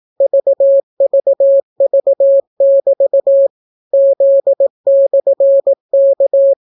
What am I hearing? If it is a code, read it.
Morse code, heard as a keyed tone, about 18 wpm, 550 Hz. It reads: VVV<BT> Z/K